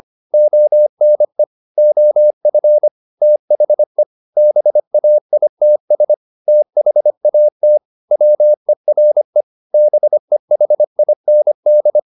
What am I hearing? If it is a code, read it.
Morse code, heard as a keyed tone, about 25 wpm, 600 Hz. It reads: ONE OF THE BAITS THAT WERE BEHIND
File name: the old man and the sea part 1 25wpm